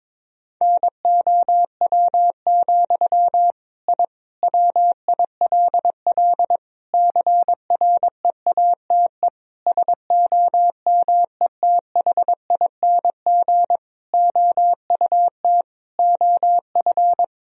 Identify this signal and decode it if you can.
Morse code, heard as a keyed tone, about 22 wpm, 700 Hz. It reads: NOW, I WILL CREATE SOMETHING OUT OF
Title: Sentences from Top 1000 Words - Review Part 3 - 22wpm